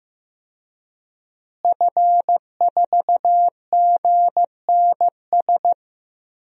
Morse code, heard as a keyed tone, about 15 wpm, 700 Hz. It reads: F4GNS
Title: DXCC-100 Call Signs - 5 Characters - Repeated 3x - 15wpm